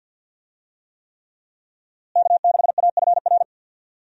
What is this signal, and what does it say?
Morse code, heard as a keyed tone, about 48 wpm, 700 Hz. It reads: K6AFR
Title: USA Call Signs - 5 Characters - Repeated 3x - 50wpm